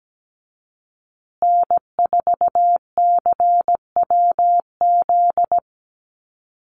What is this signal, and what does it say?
Morse code, heard as a keyed tone, about 17 wpm, 700 Hz. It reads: N4CWZ
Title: USA Call Signs Repeated 3x 17wpm